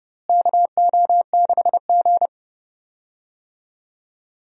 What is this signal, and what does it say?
Morse code, heard as a keyed tone, about 30 wpm, 700 Hz. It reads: KO6G
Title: USA Call Signs - 4 Characters - Repeated 3x - 30wpm